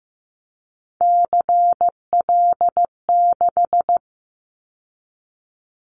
Morse code, heard as a keyed tone, about 15 wpm, 700 Hz. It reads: CL6
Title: Callsign Prefixes for Top 100 DXCC - 15wpm